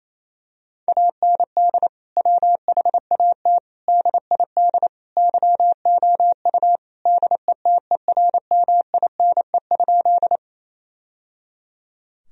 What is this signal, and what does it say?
Morse code, heard as a keyed tone, about 27 wpm, 700 Hz. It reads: AND WHAT DID YOU DETERMINE?
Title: Sentences from Top 1000 Words - Review Part 4 - 28wpm